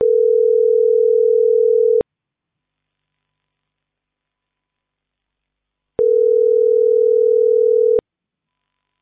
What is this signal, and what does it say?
Ringback tone (North America), 440 + 480 Hz.